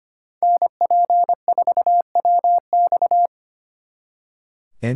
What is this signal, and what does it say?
Morse code, heard as a keyed tone, about 33 wpm, 700 Hz. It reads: NP4WX